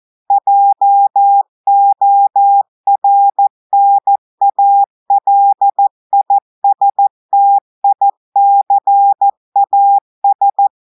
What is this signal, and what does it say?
Morse code, heard as a keyed tone, about 14 wpm, 800 Hz. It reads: JORNALISTICAS
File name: audios / portugais / portugais texte en clair 2 vitesse 14 mots minutes